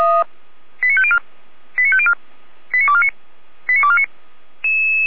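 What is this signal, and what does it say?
Five-tone selective call (ZVEI), heard as quick runs of five tones, tones of about 70 ms. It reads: 89504 89504 89249 89249
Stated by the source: ZVEI five-tone selective call received on an SDR, from feuerwehr-orlamuende.org (fire brigade Orlamünde).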